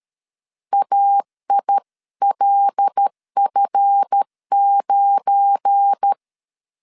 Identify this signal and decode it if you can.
Morse code, heard as a keyed tone, about 13 wpm, 800 Hz. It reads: AILF9
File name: LEZIONI / 7 GF9Z / 7 GF9 12wpm 2319